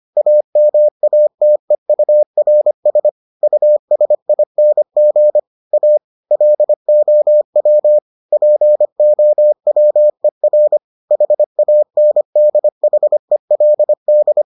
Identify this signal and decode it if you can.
Morse code, heard as a keyed tone, about 25 wpm, 600 Hz. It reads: AMATEURS USING A LOW POWER HANDHELD